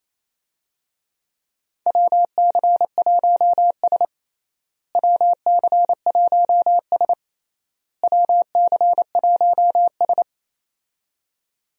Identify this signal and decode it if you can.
Morse code, heard as a keyed tone, about 28 wpm, 700 Hz. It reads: WC1S WC1S WC1S
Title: USA Call Signs - 4 Characters - Repeated 3x - 28wpm